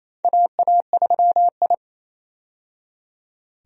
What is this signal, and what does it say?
Morse code, heard as a keyed tone, about 28 wpm, 700 Hz. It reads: AA3I
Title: USA Call Signs - 4 Characters - 28wpm